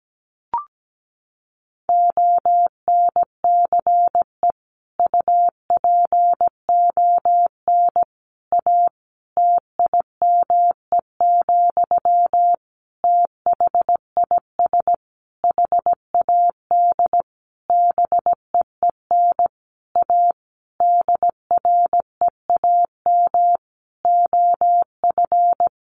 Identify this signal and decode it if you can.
Morse code, heard as a keyed tone, about 17 wpm, 700 Hz. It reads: ONCE UPON A TIME, THIS HAD BEEN A DREAM OF
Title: Sentences from Top 1500 Words 17wpm